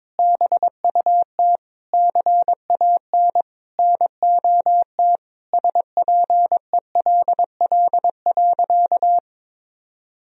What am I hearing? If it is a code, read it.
Morse code, heard as a keyed tone, about 22 wpm, 700 Hz. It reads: BUT CAN NOT SPELL.
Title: Sentences from Top 1000 Words - Review Part 1 - 22wpm